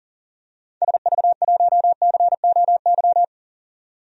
Morse code, heard as a keyed tone, about 39 wpm, 700 Hz. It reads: SV1COY
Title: DXCC-100 Call Signs - 6 Characters - 40wpm